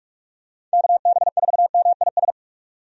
Morse code, heard as a keyed tone, about 46 wpm, 700 Hz. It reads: KB4MIS
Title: USA Call Signs - 6 Characters - 45wpm